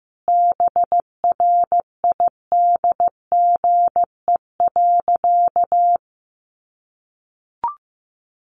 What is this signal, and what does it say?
Morse code, heard as a keyed tone, about 15 wpm, 700 Hz. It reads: BRIDGE.
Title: Sentences from Top 1200 Words 15wpm